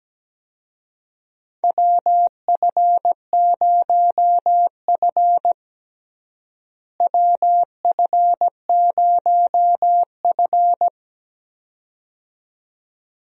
Morse code, heard as a keyed tone, about 17 wpm, 700 Hz. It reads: WF0F WF0F
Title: USA Call Signs - 4 Characters - Repeated 2x - 17wpm